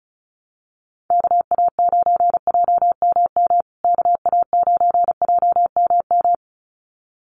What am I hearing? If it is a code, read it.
Morse code, heard as a keyed tone, about 34 wpm, 700 Hz. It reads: KA9JMM KA9JMM
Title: USA Call Signs Repeated 2x 35wpm